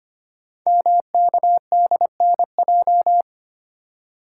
Morse code, heard as a keyed tone, about 25 wpm, 700 Hz. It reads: MKDNJ
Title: Code Groups - 5 Letters - 25wpm